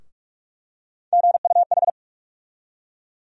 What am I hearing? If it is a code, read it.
Morse code, heard as a keyed tone, about 46 wpm, 700 Hz. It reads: GUH
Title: USA Call Sign Postfixes - 3 Letters 45wpm